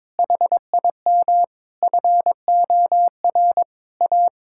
Morse code, heard as a keyed tone, about 21 wpm, 700 Hz. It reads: HIM FOR A
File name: practice / 22wpm / Sentences with 4-Word Phrases - 22wpm